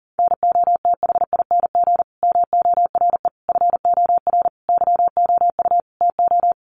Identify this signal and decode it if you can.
Morse code, heard as a keyed tone, about 40 wpm, 700 Hz. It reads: NOTHING MORE FOR YOU TO
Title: Sentences from Top 1100 Words 40wpm